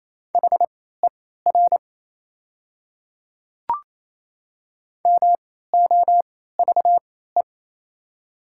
Morse code, heard as a keyed tone, about 28 wpm, spaced as for 16 wpm, 700 Hz. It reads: HER MOVE
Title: 3x Character Spacing - Top 200 Words - 28wpm